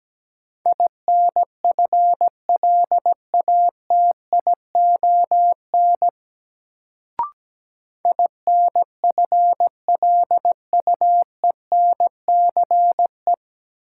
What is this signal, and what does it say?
Morse code, heard as a keyed tone, about 17 wpm, 700 Hz. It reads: INFLATION INFLUENCE
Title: Sentences from Top 1100 Words - New Word Drill - 17wpm